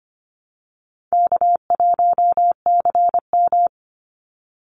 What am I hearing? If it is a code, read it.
Morse code, heard as a keyed tone, about 25 wpm, 700 Hz. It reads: K1CM
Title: NA SOTA Chaser Callsigns 25wpm